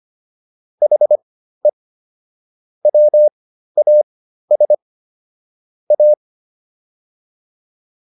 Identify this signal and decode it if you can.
Morse code, heard as a keyed tone, about 25 wpm, spaced as for 13 wpm, 600 Hz. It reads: HE WAS A